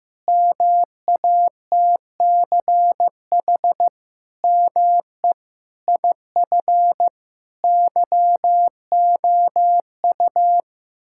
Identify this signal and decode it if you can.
Morse code, heard as a keyed tone, about 15 wpm, 700 Hz. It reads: MATCH ME IF YOU
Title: Sentences from Top 1600 Words 15wpm